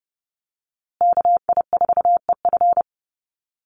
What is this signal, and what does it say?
Morse code, heard as a keyed tone, about 30 wpm, 700 Hz. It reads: KI4EF
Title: USA Call Signs Repeated 2x 30wpm